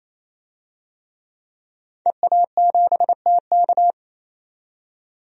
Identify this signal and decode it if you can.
Morse code, heard as a keyed tone, about 27 wpm, 700 Hz. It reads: EA7TK
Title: DXCC-100 Call Signs - 5 Characters - 28wpm